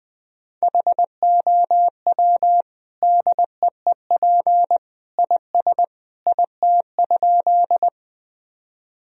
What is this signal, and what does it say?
Morse code, heard as a keyed tone, about 20 wpm, 700 Hz. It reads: HOW DEEP IS IT?
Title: Sentences from Top 1000 Words - Review Part 3 - 20wpm